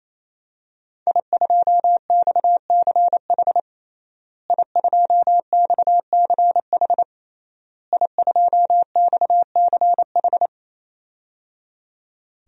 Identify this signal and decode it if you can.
Morse code, heard as a keyed tone, about 28 wpm, 700 Hz. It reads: I2XCH I2XCH I2XCH
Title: DXCC-100 Call Signs - 5 Characters - Repeated 3x - 28wpm